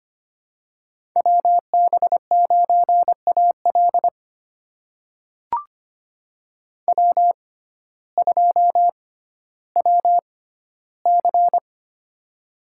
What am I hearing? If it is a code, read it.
Morse code, heard as a keyed tone, about 25 wpm, spaced as for 14 wpm, 700 Hz. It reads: WB9AL W2WC
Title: USA Call Signs - 6x Character Spacing - 25wpm